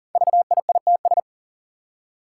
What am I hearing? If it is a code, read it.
Morse code, heard as a keyed tone, about 40 wpm, 700 Hz. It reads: VIITS
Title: Five Challenging Letter-Number Combinations 40wpm